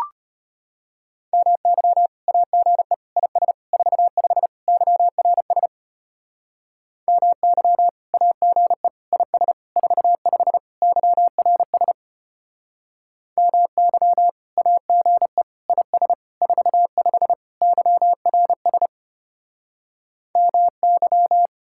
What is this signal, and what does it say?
Morse code, heard as a keyed tone, about 32 wpm, 700 Hz. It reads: MY AGE IS 45 YRS MY AGE IS 45 YRS MY AGE IS 45 YRS MY